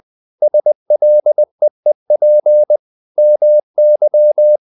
Morse code, heard as a keyed tone, about 20 wpm, 600 Hz. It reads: SLEEP MY
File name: the old man and the sea part 2 20wpm